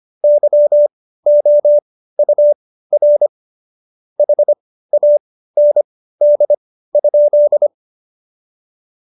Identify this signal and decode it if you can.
Morse code, heard as a keyed tone, about 25 wpm, spaced as for 15 wpm, 590 Hz. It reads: YOUR HAND?
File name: Jokes-25-char-15-eff